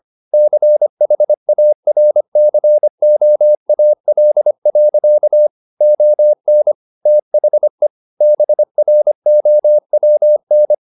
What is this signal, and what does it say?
Morse code, heard as a keyed tone, about 25 wpm, 600 Hz. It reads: CHARCOAL. ON THE BROWN
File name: the old man and the sea part 1 25wpm